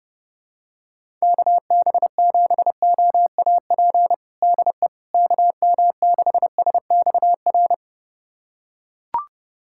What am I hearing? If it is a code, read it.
Morse code, heard as a keyed tone, about 30 wpm, 700 Hz. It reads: KB7OAP DE KM6SXR
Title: USA Callsign DE Callsign - 6 Characters - 30wpm